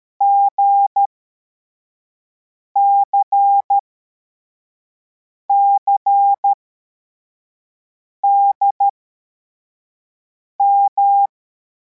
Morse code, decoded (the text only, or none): GCCDM